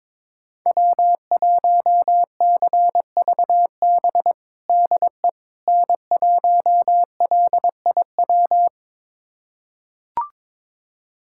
Morse code, heard as a keyed tone, about 22 wpm, 700 Hz. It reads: W1CVB DE N1LIW
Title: USA Callsign DE Callsign - 5 Characters - 22wpm